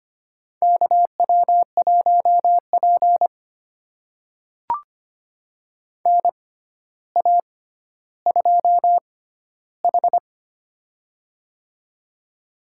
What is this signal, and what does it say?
Morse code, heard as a keyed tone, about 25 wpm, spaced as for 13 wpm, 700 Hz. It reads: KW1P NA2H
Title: USA Call Signs - 6x Character Spacing - 25wpm